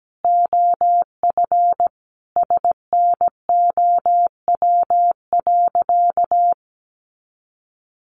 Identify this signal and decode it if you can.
Morse code, heard as a keyed tone, about 17 wpm, 700 Hz. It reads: OF SNOW.
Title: Easy Phrases 17wpm